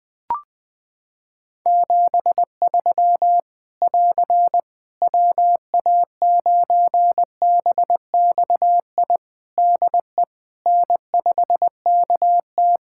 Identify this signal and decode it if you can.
Morse code, heard as a keyed tone, about 20 wpm, 700 Hz. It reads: 73 <AR> WA9BXI DE N5KT
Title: Practice QSO - Exchange 4 - Ending - 20wpm